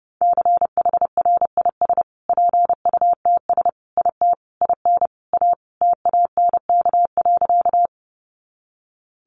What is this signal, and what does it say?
Morse code, heard as a keyed tone, about 30 wpm, 700 Hz. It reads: CHRIS PUTS IT IN A TANK.
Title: Easy Phrases 30wpm